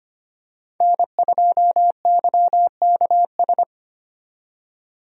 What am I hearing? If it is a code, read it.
Morse code, heard as a keyed tone, about 25 wpm, 700 Hz. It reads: N2YKS
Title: USA Call Signs - 5 Characters - 25wpm